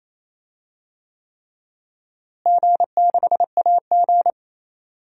Morse code, heard as a keyed tone, about 28 wpm, 700 Hz. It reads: G6AG
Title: DXCC-100 Call Signs - 4 Characters - Repeated 3x - 28wpm